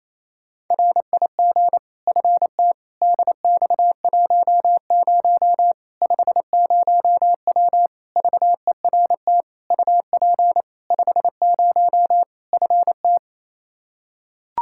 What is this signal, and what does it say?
Morse code, heard as a keyed tone, about 28 wpm, 700 Hz. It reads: RIG FT DX10 50W VERT UP 50 FT
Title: Practice QSO - Exchange 2 - Info - 28wpm